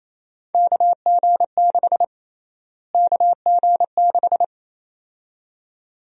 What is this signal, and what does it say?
Morse code, heard as a keyed tone, about 28 wpm, 700 Hz. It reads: KG6 KG6
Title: USA Call Sign Prefixes - Repeated 2x - 28wpm